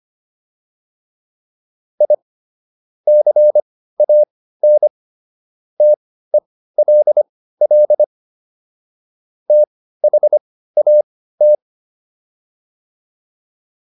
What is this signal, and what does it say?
Morse code, heard as a keyed tone, about 25 wpm, spaced as for 15 wpm, 600 Hz. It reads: I CAN TELL THAT